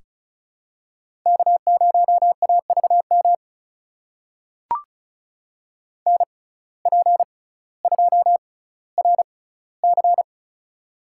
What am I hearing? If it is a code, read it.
Morse code, heard as a keyed tone, about 36 wpm, spaced as for 18 wpm, 700 Hz. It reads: K0AVM NP2RC
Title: USA Call Signs - 5 Characters - 6x Character Spacing - 35wpm